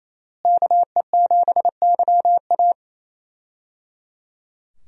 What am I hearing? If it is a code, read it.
Morse code, heard as a keyed tone, about 28 wpm, 700 Hz. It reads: KE7YA
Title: USA Call Signs - 5 Characters - Repeated 3x - 28wpm